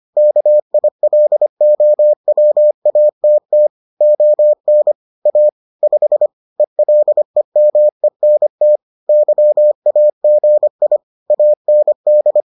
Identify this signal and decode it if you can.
Morse code, heard as a keyed tone, about 25 wpm, 600 Hz. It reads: KILOWATT ON A 5 ELEMENT YAGI AND